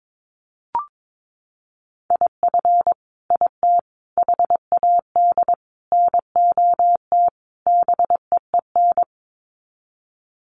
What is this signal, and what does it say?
Morse code, heard as a keyed tone, about 22 wpm, 700 Hz. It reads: IF IT HAD NOT BEEN
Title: Sets of 5 Words Encore 22wpm